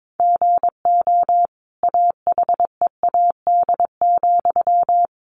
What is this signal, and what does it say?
Morse code, heard as a keyed tone, about 22 wpm, 700 Hz. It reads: GO AHEAD,
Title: Sentences from Top 1200 Words 22wpm